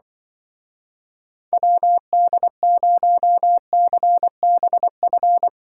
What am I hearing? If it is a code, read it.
Morse code, heard as a keyed tone, about 24 wpm, 700 Hz. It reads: WD0CBF
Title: Speed Racing - USA Call Signs 20wpm